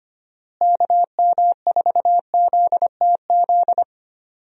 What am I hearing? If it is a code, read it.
Morse code, heard as a keyed tone, about 25 wpm, 700 Hz. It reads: KM4ZTZ
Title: USA Call Signs - 6 Characters - Repeated 3x - 25wpm